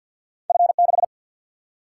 Morse code, heard as a keyed tone, about 51 wpm, 700 Hz. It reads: F 6
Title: Single-Letter Single-Number 50wpm